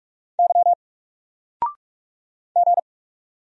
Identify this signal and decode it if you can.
Morse code, heard as a keyed tone, about 44 wpm, 700 Hz. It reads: Y G
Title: Single Character (Letter:Number:Punctuation:Prosign) 45wpm